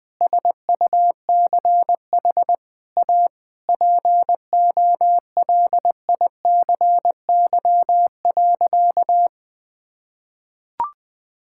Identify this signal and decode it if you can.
Morse code, heard as a keyed tone, about 20 wpm, 700 Hz. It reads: SUCH A POLICY.
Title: Sentences from Top 1000 Words - Review Part 6 - 20wpm